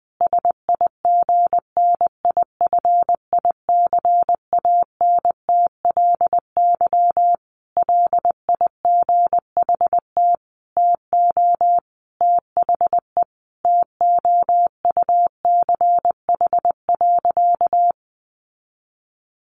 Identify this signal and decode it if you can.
Morse code, heard as a keyed tone, about 20 wpm, 700 Hz. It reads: SIGNIFICANTLY LIGHT TO THE TOUCH.
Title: Sentences from Top 1500 Words 20wpm